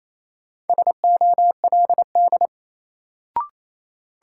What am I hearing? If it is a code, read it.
Morse code, heard as a keyed tone, about 28 wpm, 700 Hz. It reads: SOLD
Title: Sentences from Top 1400 Words - New Word Drill Alt - 28wpm